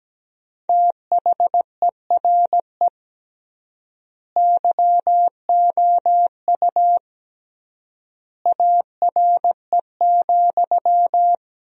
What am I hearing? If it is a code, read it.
Morse code, heard as a keyed tone, about 17 wpm, 700 Hz. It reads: THERE YOU ARE,
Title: 3x Word-Spacing - Sentences from Top 100 Words - 17wpm